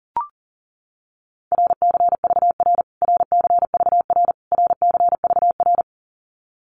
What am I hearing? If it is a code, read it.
Morse code, heard as a keyed tone, about 38 wpm, 700 Hz. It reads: RCVR RCVR RCVR
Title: Abbreviations Repeated 3x - 40wpm